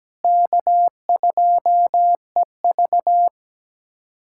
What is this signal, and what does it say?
Morse code, heard as a keyed tone, about 17 wpm, 700 Hz. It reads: K2EV
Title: USA Callsign DE Callsign - 4 Characters - 17wpm